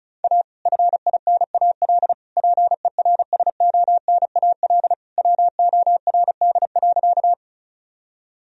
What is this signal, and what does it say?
Morse code, heard as a keyed tone, about 36 wpm, 700 Hz. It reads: A FINAL PERSONAL WORD.
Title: Sentences from Top 1000 Words - Review Part 4 - 35wpm